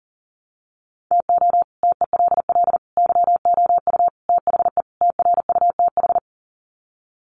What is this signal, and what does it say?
Morse code, heard as a keyed tone, about 37 wpm, 700 Hz. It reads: TO TELL YOU THE TRUTH
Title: Sets of 5 Words 40wpm